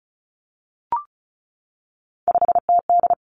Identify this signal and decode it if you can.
Morse code, heard as a keyed tone, about 36 wpm, 700 Hz. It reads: 5TD
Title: Three Challenging Letter-Number Combinations 35wpm